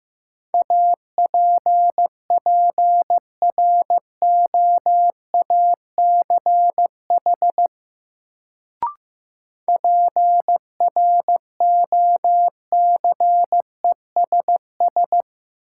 Morse code, heard as a keyed tone, about 15 wpm, 700 Hz. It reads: APPROACH PROCESS
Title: Sentences from Top 1400 Words - New Word Drill Alt - 15wpm